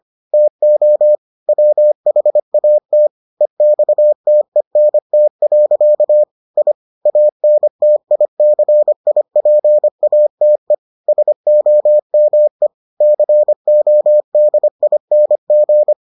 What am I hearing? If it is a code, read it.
Morse code, heard as a keyed tone, about 25 wpm, 600 Hz. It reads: TO WHAT EXTENT. I ANTICIPATE SOME CODING